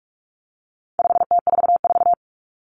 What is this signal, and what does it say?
Morse code, heard as a keyed tone, about 44 wpm, 700 Hz. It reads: 5T44
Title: Common Cut Numbers - 4-Digit 45wpm